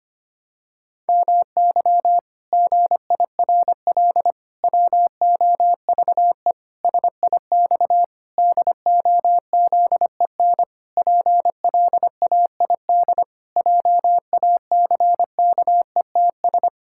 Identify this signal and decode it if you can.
Morse code, heard as a keyed tone, about 25 wpm, 700 Hz. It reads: MY GIRL WOVE SIX DOZEN PLAID JACKETS
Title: Pangram Challenge v2 - 25wpm